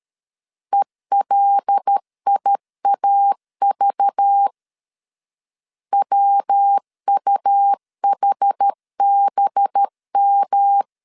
Morse code, heard as a keyed tone, about 13 wpm, 800 Hz. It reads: ELIAV WUHBM
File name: LEZIONI / 6 PKX0 / 6 PKX0 12wpm 2231